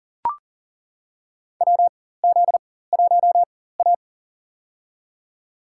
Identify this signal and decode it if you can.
Morse code, heard as a keyed tone, about 40 wpm, spaced as for 19 wpm, 700 Hz. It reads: WZ1A
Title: USA Call Signs - 4 Characters - 4x Character Spacing - 40wpm